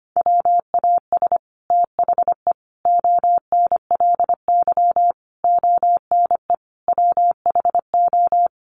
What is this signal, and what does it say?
Morse code, heard as a keyed tone, about 25 wpm, 700 Hz. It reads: WAS THE ONLY ONE WHO
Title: Sets of 5 Words Encore2 25wpm